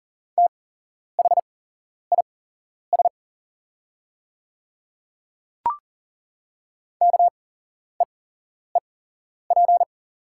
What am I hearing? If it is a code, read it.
Morse code, heard as a keyed tone, about 40 wpm, spaced as for 11 wpm, 690 Hz. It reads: THIS KEEP
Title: 8x Character Spacing - Top 300 Words - 40wpm